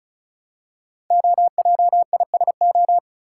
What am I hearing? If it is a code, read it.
Morse code, heard as a keyed tone, about 36 wpm, 700 Hz. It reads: OJISO
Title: Code Groups - 5 Letters - 35wpm